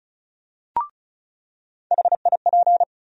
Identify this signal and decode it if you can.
Morse code, heard as a keyed tone, about 36 wpm, 700 Hz. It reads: HIP
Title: USA Call Sign Postfixes - 3 Letters 35wpm